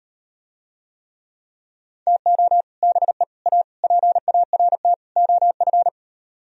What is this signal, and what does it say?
Morse code, heard as a keyed tone, about 38 wpm, 700 Hz. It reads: TO BE A PART OF